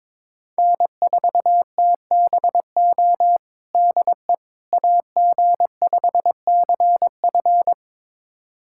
Morse code, heard as a keyed tone, about 21 wpm, 700 Hz. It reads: N4TBO DE AG5CF
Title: USA Callsign DE Callsign - 5 Characters - 22wpm